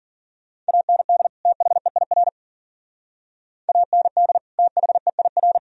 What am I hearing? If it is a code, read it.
Morse code, heard as a keyed tone, about 50 wpm, 700 Hz. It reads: AND THEIR AND THEIR
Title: Speed-Racing - Sets of 2 Words 40wpm